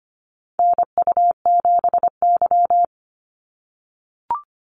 Morse code, heard as a keyed tone, about 25 wpm, 700 Hz. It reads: NU7Y
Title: NA SOTA Activator Callsigns 25wpm